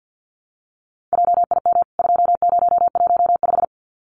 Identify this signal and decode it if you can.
Morse code, heard as a keyed tone, about 50 wpm, 700 Hz. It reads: JIM 2015
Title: CWT Exchange 50wpm